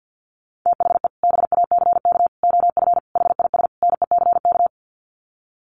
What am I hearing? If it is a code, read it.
Morse code, heard as a keyed tone, about 50 wpm, 700 Hz. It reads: THE BACK OF HIS NECK
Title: Sets of 5 Words Encore 50wpm